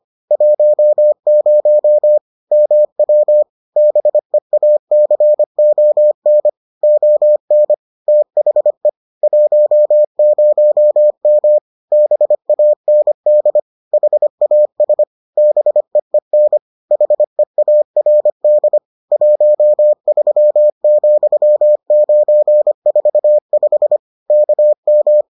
Morse code, heard as a keyed tone, about 25 wpm, 600 Hz. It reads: 10 MW BEACON ON THE 10M BAND HAS BEEN HEARD 13,945 KM